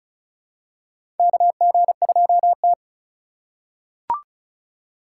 Morse code, heard as a keyed tone, about 36 wpm, 700 Hz. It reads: KG2T
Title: USA Call Signs - 4 Characters - 3x Character Spacing - 35wpm